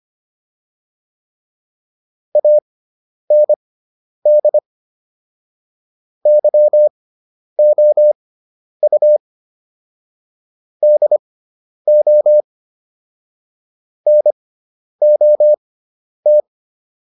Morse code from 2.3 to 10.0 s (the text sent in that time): AND YOU